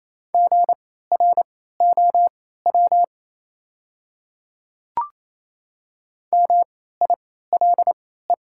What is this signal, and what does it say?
Morse code, heard as a keyed tone, about 28 wpm, spaced as for 16 wpm, 700 Hz. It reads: GROW MILE
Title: 3x Character Spacing - Top 300 Words - 28wpm